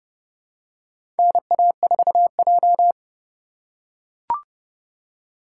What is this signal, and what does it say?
Morse code, heard as a keyed tone, about 30 wpm, 700 Hz. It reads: NA4J
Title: USA Call Signs - 4 Characters - 4x Character Spacing - 30wpm